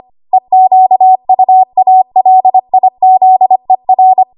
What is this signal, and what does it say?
Morse code, heard as a keyed tone, about 25 wpm, 750 Hz. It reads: EQUALIZER
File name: Morse / 25WPM